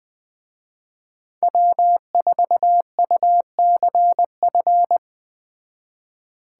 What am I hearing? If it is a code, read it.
Morse code, heard as a keyed tone, about 20 wpm, 700 Hz. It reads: W4UCF
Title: Speed Racing - USA 1x3 Call Signs 17wpm